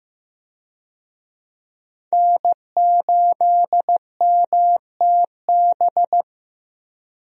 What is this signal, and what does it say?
Morse code, heard as a keyed tone, about 16 wpm, 700 Hz. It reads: N8MTB
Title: USA Call Signs - 5 Characters - Repeated 3x - 15wpm